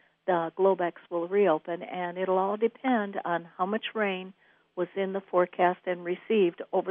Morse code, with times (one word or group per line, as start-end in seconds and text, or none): none